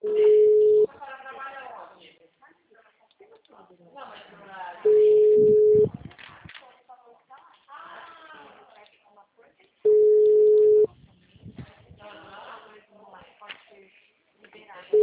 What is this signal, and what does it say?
Ringback tone (Europe, 425 Hz), with speech.